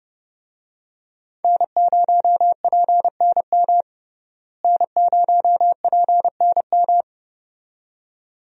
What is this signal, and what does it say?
Morse code, heard as a keyed tone, about 30 wpm, 700 Hz. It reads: N0PNM N0PNM
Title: USA Call Signs - 5 Characters - Repeated 2x - 30wpm